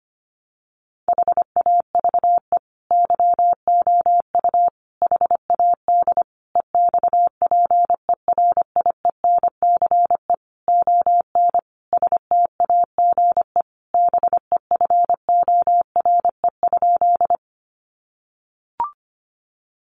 Morse code, heard as a keyed tone, about 25 wpm, 700 Hz. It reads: HAVE YOU HAD EXPERIENCE ON STAGE BEFORE?
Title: Sentences from Top 1200 Words 25wpm